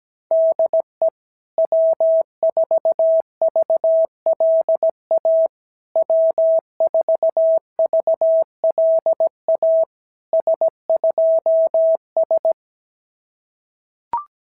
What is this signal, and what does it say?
Morse code, heard as a keyed tone, about 17 wpm, 650 Hz. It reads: DE W4VLA W4VLA S2S
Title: SOTA QSO Practice 17wpm